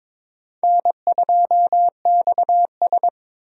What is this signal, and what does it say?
Morse code, heard as a keyed tone, about 22 wpm, 700 Hz. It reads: N2XS
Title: USA Call Signs - 4 Characters - Repeated 2x - 22wpm